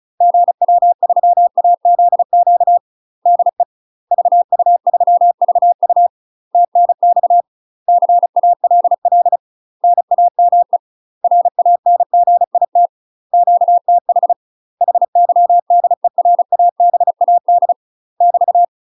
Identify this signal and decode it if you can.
Morse code, heard as a keyed tone, about 36 wpm, 700 Hz. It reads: GW3AZQ DE VU3VU TNX CALL NAME RANGIT QTH HYDERABAD <BT>
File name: UK MASTER QSOs TRAINING / 35WPM / UK MASTER QSOs 0023